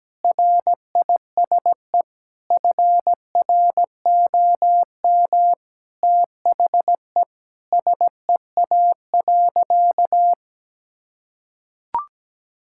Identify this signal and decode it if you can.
Morse code, heard as a keyed tone, about 17 wpm, 700 Hz. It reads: RISE FROM THE SEA.
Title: Sentences from Top 1000 Words - Review Part 6 - 17wpm